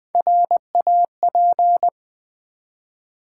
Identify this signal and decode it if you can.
Morse code, heard as a keyed tone, about 20 wpm, 700 Hz. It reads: RAP
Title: Build A Word - 1 Letter At A Time - 4-Letter Words - 20wpm